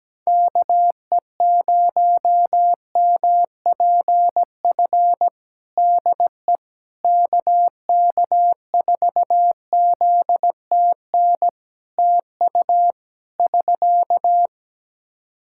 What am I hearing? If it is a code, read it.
Morse code, heard as a keyed tone, about 17 wpm, 700 Hz. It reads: KE0MPF DE KK4ZTN TU <SK>
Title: Practice QSO - Exchange 4 - Ending - 17wpm